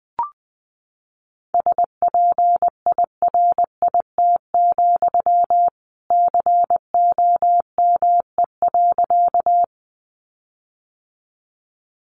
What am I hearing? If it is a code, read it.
Morse code, heard as a keyed tone, about 19 wpm, 700 Hz. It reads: SPIRIT, COME.
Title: Sentences from Top 1400 Words 20wpm